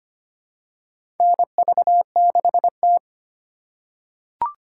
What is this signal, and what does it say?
Morse code, heard as a keyed tone, about 25 wpm, 700 Hz. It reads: NV6T
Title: USA Call Signs - 2x Character Spacing - 25wpm